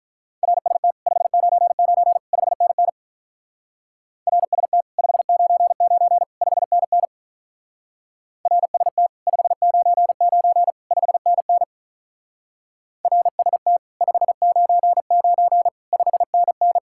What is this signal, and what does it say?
Morse code, heard as a keyed tone, about 46 wpm, 700 Hz. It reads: RST 599 5NN RST 599 5NN RST 599 5NN RST 599 5NN